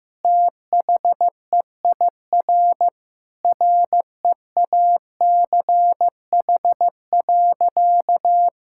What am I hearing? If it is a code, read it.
Morse code, heard as a keyed tone, about 16 wpm, 700 Hz. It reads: THEIR REACH.
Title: Sentences from Top 1000 Words - Review Part 6 - 15wpm